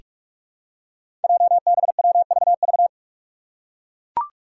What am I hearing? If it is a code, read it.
Morse code, heard as a keyed tone, about 44 wpm, 700 Hz. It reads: JBWVV
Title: Five Challenging Letter-Number Combinations 45wpm